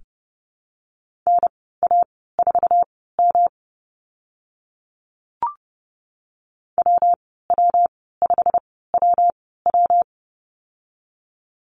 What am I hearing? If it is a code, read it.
Morse code, heard as a keyed tone, about 30 wpm, spaced as for 17 wpm, 700 Hz. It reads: NA4M WW5WW
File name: Lesson 19g - Letter w - Call Signs 30wpm